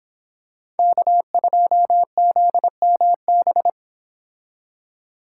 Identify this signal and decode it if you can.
Morse code, heard as a keyed tone, about 26 wpm, 700 Hz. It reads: K2ZMB